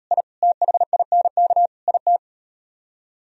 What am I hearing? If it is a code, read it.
Morse code, heard as a keyed tone, about 38 wpm, 700 Hz. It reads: I THINK IT